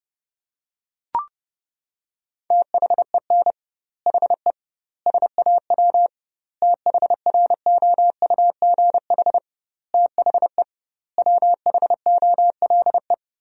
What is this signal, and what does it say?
Morse code, heard as a keyed tone, about 32 wpm, 700 Hz. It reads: THEN HE SAW THROUGH THE WHOLE
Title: 2x Word-Spacing - Sentences from Top 400 Words - 30wpm